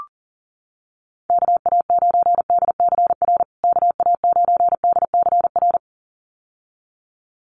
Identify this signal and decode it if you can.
Morse code, heard as a keyed tone, about 40 wpm, 700 Hz. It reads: KA9DCR KA9DCR
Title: USA Call Signs Repeated 2x 40wpm